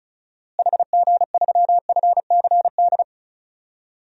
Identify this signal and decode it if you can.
Morse code, heard as a keyed tone, about 34 wpm, 700 Hz. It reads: HG3FCD
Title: DXCC-100 Call Signs - 6 Characters - Repeated 2x - 35wpm